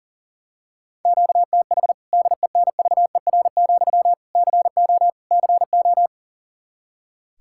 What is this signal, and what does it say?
Morse code, heard as a keyed tone, about 40 wpm, 700 Hz. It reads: QTH DENVER, CO CO
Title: Practice QSO - Exchange 1 - Beginning - 40wpm